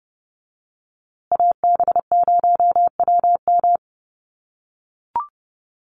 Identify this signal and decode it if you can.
Morse code, heard as a keyed tone, about 30 wpm, 700 Hz. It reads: AB0WM
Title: USA Call Signs Repeated 2x 30wpm